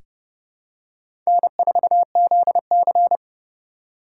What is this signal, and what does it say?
Morse code, heard as a keyed tone, about 31 wpm, 700 Hz. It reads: N4ZC
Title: USA Call Signs - 4 Characters - 3x Character Spacing - 30wpm